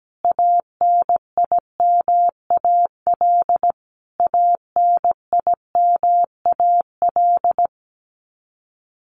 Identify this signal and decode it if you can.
Morse code, heard as a keyed tone, about 17 wpm, 700 Hz. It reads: ANIMAL ANIMAL
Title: Top 500 Words Repeated 17wpm